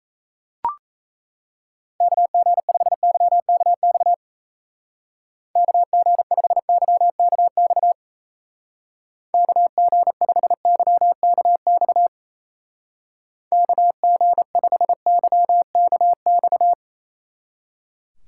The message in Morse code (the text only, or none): KG5YKX KG5YKX KG5YKX KG5YKX